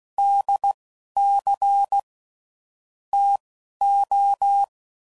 Morse code, heard as a keyed tone, about 16 wpm, spaced as for 11 wpm, 800 Hz. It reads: DC TO
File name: CW Code Groups @10WPM / Technical Subjects / SwPwrSup